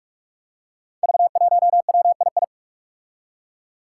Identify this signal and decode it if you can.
Morse code, heard as a keyed tone, about 46 wpm, 700 Hz. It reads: V1WII
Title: Five Challenging Letter-Number Combinations 45wpm